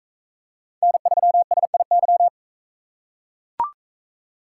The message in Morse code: N3SIY

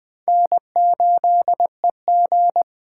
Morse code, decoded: N8EG

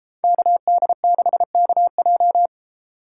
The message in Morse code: KD6KJ

